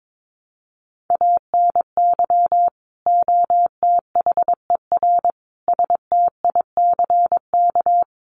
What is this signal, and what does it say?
Morse code, heard as a keyed tone, about 22 wpm, 700 Hz. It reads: ANY OTHER STICK
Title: Sentences from Top 1200 Words 22wpm